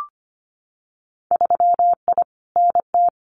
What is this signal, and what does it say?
Morse code, heard as a keyed tone, about 25 wpm, 700 Hz. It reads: 3I NT
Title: Winter Field Day Exchanges Encore 25wpm